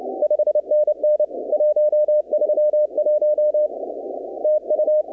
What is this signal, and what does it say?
Morse code, heard as a keyed tone, about 27 wpm, 610 Hz. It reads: 5NN 131 TU
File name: contests / 2019-cw-wpx-kt5j-k5pi- -2330 0-20